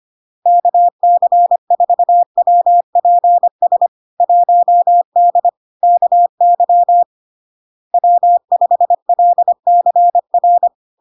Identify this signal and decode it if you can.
Morse code, heard as a keyed tone, about 25 wpm, 700 Hz. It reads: KC4WPS 1D KY W5LCR
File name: LARK-FD-2022-Log-in-CW-25wpm-700-Hz